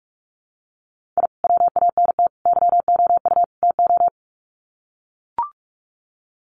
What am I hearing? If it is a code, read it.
Morse code, heard as a keyed tone, about 45 wpm, 700 Hz. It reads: I WANT YOU TO
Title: Sets of 4 Words 45wpm